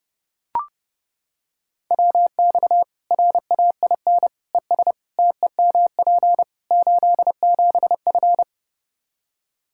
Morse code, heard as a keyed tone, about 30 wpm, 700 Hz. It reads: WX RAIN ES TEMP 87F
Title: Practice QSO - Exchange 2 - Info - 30wpm